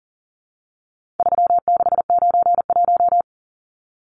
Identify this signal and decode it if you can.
Morse code, heard as a keyed tone, about 40 wpm, 700 Hz. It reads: 3691
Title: Numbers - 4-Digit 40wpm